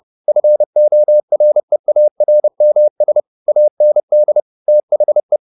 Morse code, heard as a keyed tone, about 30 wpm, 600 Hz. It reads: FOREARMS AND THE